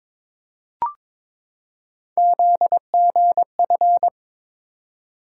Morse code, heard as a keyed tone, about 22 wpm, 700 Hz. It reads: ZGF
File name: practice / 22wpm / Three Letter-Number Combinations - 22wpm